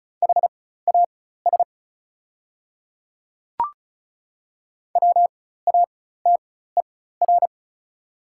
Morse code, heard as a keyed tone, about 36 wpm, spaced as for 16 wpm, 700 Hz. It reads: HAS WATER
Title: 4x Character Spacing - Top 100 Words - 35wpm